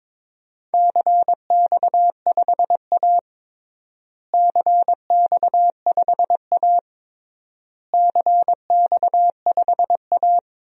Morse code, heard as a keyed tone, about 22 wpm, 700 Hz. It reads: CX5A CX5A CX5A
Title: DXCC-100 Call Signs - 4 Characters - Repeated 3x - 22wpm